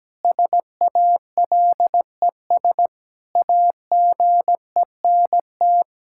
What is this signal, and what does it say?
Morse code, heard as a keyed tone, about 17 wpm, 700 Hz. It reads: SALES AGENT
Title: Practice QSO - Exchange 3 - More Info - 17wpm